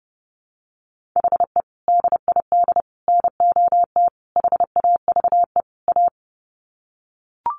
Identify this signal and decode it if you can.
Morse code, heard as a keyed tone, about 30 wpm, 700 Hz. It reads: HE DID NOT HAVE A
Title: Sets of 5 Words Encore 30wpm